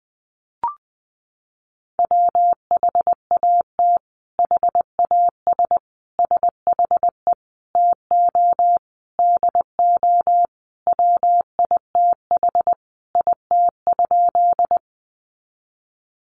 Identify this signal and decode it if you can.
Morse code, heard as a keyed tone, about 19 wpm, 700 Hz. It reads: WHAT HAS SHE TO DO WITH IT?
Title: Sentences from Top 100 Words 20wpm